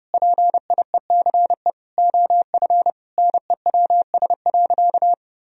Morse code, heard as a keyed tone, about 30 wpm, 700 Hz. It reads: PIECE OF NEWS.
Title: Sentences from Top 1000 Words - Review Part 2 - 30wpm